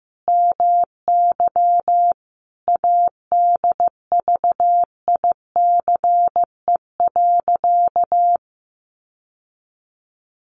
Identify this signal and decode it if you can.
Morse code, heard as a keyed tone, about 15 wpm, 700 Hz. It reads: MY ADVICE.
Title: Sentences from Top 1100 Words 15wpm